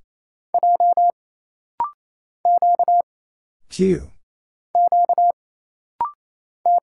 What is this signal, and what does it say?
Morse code, heard as a keyed tone, about 27 wpm, 700 Hz. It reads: J Q Q T
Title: Single Letters - Rapid-Fire 28wpm